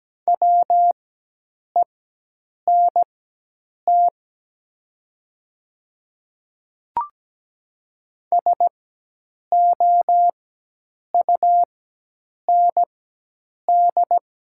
Morse code, heard as a keyed tone, about 17 wpm, spaced as for 8 wpm, 700 Hz. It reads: WENT SOUND
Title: 4x Character Spacing - Top 300 Words - 17wpm